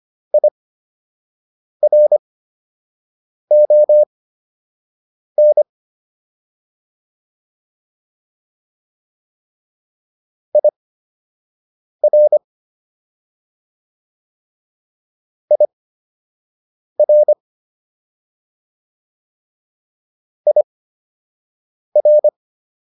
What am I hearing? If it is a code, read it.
Morse code, heard as a keyed tone, about 25 wpm, spaced as for 6 wpm, 600 Hz. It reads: IRON IR IR IR